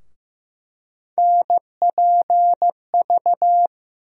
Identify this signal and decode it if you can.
Morse code, heard as a keyed tone, about 15 wpm, 700 Hz. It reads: NPV
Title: USA Call Sign Postfixes - 3 Letters 15wpm